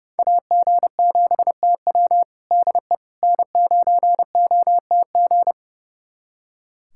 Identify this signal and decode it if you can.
Morse code, heard as a keyed tone, about 30 wpm, 700 Hz. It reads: AG7TW DE N9OTG
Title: USA Callsign DE Callsign - 5 Characters - 30wpm